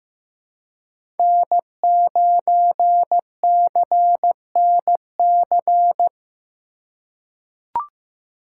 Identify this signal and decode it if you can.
Morse code, heard as a keyed tone, about 15 wpm, 700 Hz. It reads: N9CNC
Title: USA Call Signs - 5 Characters - 3x Character Spacing - 15wpm